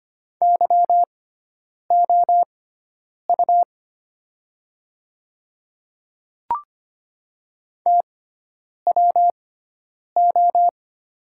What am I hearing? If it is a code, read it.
Morse code, heard as a keyed tone, about 25 wpm, spaced as for 9 wpm, 700 Hz. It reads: YOU TWO
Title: 6x Character Spacing - Top 100 Words - 25wpm